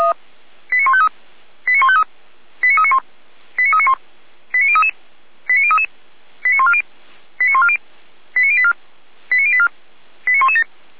Five-tone selective call (ZVEI), heard as quick runs of five tones, tones of about 70 ms. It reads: 89153 89153 89391 89391 89033 89033 89140 89140 89084 89084 89117